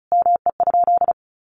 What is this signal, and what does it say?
Morse code, heard as a keyed tone, about 34 wpm, 700 Hz. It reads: ME?
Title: Sentences from Top 900 Words 35wpm